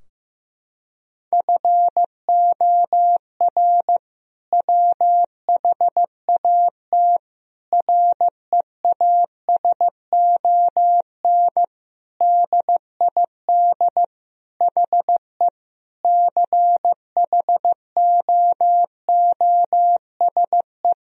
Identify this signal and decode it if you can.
Morse code, heard as a keyed tone, about 15 wpm, 700 Hz. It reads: FOR WHAT REASON DID HE CHOOSE
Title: Sentences from Top 1000 Words - Review Part 4 - 15wpm